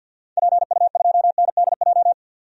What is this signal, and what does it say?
Morse code, heard as a keyed tone, about 51 wpm, 700 Hz. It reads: PU2NDJ
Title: DXCC-100 Call Signs - 6 Characters - 50wpm